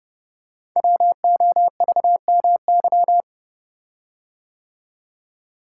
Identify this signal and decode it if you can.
Morse code, heard as a keyed tone, about 28 wpm, 700 Hz. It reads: WOVMY
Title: Code Groups - 5 Letters - 30wpm